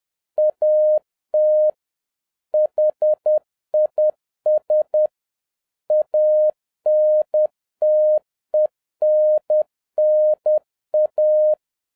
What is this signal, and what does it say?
Morse code, heard as a keyed tone, about 10 wpm, 610 Hz. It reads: AT HIS ANTENNA
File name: 2011-07-mp3 / n46 JOHN K3TN 10wpm